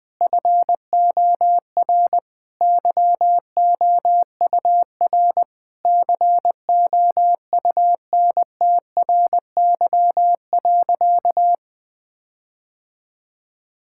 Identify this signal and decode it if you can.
Morse code, heard as a keyed tone, about 20 wpm, 700 Hz. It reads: FOR YOUR COUNTRY.
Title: Sentences with 9-Word Phrases - Spoken Beginning - 20wpm